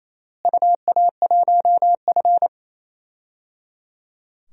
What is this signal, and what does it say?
Morse code, heard as a keyed tone, about 27 wpm, 700 Hz. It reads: UA1F
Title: DXCC-100 Call Signs - 4 Characters - Repeated 3x - 28wpm